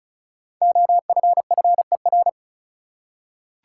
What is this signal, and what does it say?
Morse code, heard as a keyed tone, about 34 wpm, 700 Hz. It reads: OFFER
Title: Five Letter Words - 35wpm